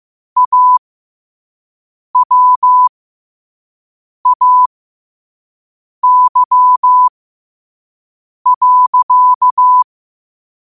Morse code, heard as a keyed tone, about 15 wpm, spaced as for 5 wpm, 1000 Hz. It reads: AWAY.
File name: qotdfiles / 5wpm